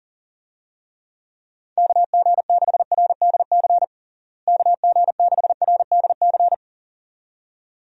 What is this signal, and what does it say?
Morse code, heard as a keyed tone, about 40 wpm, 700 Hz. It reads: KG6RDC KG6RDC
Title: USA Call Signs - 6 Characters - Repeated 2x - 40wpm